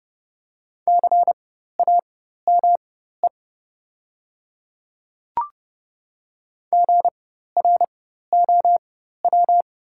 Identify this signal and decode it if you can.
Morse code, heard as a keyed tone, about 30 wpm, spaced as for 14 wpm, 700 Hz. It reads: CAME GROW
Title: 4x Character Spacing - Top 300 Words - 30wpm